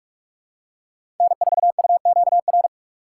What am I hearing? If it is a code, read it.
Morse code, heard as a keyed tone, about 43 wpm, 700 Hz. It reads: N4UQR
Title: USA Call Signs - 5 Characters - 6x Character Spacing - 45wpm